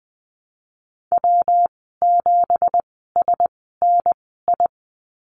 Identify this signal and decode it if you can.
Morse code, heard as a keyed tone, about 20 wpm, spaced as for 15 wpm, 700 Hz. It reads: W7SNI
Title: 167 Lesson 27g - Number 7 - Call Signs 20wpm